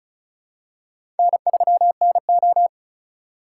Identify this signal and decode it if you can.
Morse code, heard as a keyed tone, about 36 wpm, 700 Hz. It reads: N3NO
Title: USA Call Signs - 3x Character Spacing - 35wpm